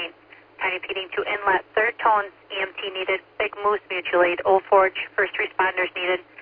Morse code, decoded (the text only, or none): none